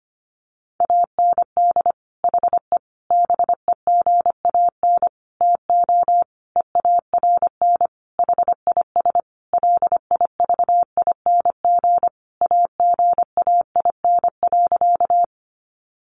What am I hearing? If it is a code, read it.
Morse code, heard as a keyed tone, about 25 wpm, 700 Hz. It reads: AND HE BEGAN TO EARN HIS LIVING AGAIN.
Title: Sentences from Top 1500 Words 25wpm